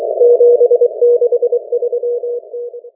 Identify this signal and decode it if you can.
Morse code, heard as a keyed tone, about 23 wpm, 480 Hz. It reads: 763T